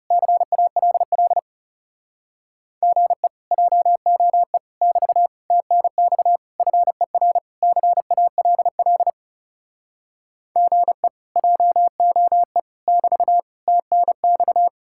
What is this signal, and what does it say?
Morse code, heard as a keyed tone, about 40 wpm, 700 Hz. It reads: CALL GE JOE <BT> TNX FER CALL GE JOE <BT> TNX